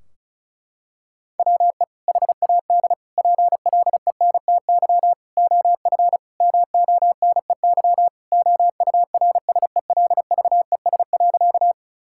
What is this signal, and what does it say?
Morse code, heard as a keyed tone, about 36 wpm, 700 Hz. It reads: WE HAD PLENTY OF MONEY OURSELVES.
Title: Sentences from Top 1600 Words 35wpm